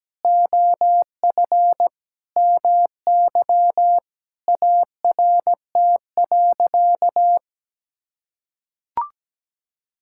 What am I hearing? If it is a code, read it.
Morse code, heard as a keyed tone, about 17 wpm, 700 Hz. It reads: OF MY ART.
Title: Sentences from Top 1000 Words - Review Part 6 - 17wpm